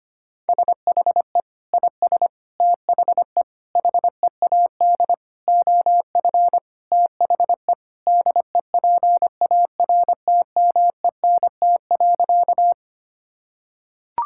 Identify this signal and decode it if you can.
Morse code, heard as a keyed tone, about 25 wpm, 700 Hz. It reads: SHE IS THE HEAD OF THE DEPARTMENT.
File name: practice / 25wpm / Sentences with 3-Word Phrases - 25wpm